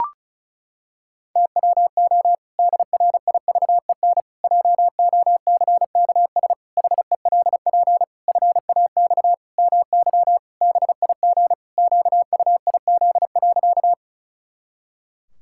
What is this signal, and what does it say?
Morse code, heard as a keyed tone, about 36 wpm, 700 Hz. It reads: TWO DRIVEN JOCKS HELP FAX MY BIG QUIZ.
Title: Pangram Challenge v2 - 35wpm